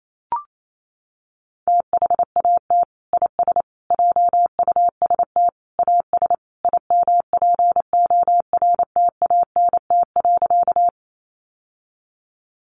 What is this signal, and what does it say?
Morse code, heard as a keyed tone, about 28 wpm, 700 Hz. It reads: THAT IS JUST AS IMPORTANT.
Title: Sentences from Top 600 Words 28wpm